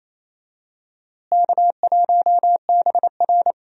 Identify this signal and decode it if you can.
Morse code, heard as a keyed tone, about 28 wpm, 700 Hz. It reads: K1BR
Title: USA Call Signs - 4 Characters - Repeated 3x - 28wpm